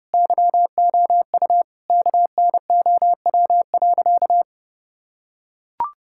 Morse code, heard as a keyed tone, about 30 wpm, 700 Hz. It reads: YOU KNOW.
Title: Sentences from Top 1000 Words - Review Part 5 - 30wpm